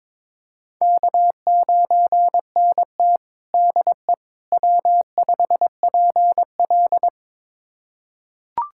Morse code, heard as a keyed tone, about 22 wpm, 700 Hz. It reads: K9NT DE W5PL
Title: USA Callsign DE Callsign - 4 Characters - 22wpm